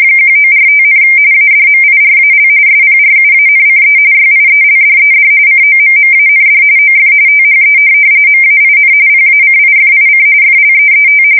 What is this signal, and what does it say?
RTTY, about 45.45 baud. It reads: 1945, ADMIRAL SPRUANCE YIELDED COMMAND OF THE MAIN PACIFIC FLEET
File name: fsk170 / USS-New-Jersey